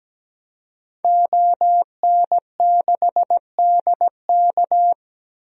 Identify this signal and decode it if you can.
Morse code, heard as a keyed tone, about 17 wpm, 700 Hz. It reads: ON6DK
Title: DXCC-100 Call Signs - 5 Characters - Repeated 3x - 17wpm